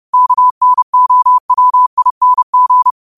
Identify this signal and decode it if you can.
Morse code, heard as a keyed tone, about 30 wpm, 1000 Hz. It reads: KNOWING
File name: qotdfiles / 30wpm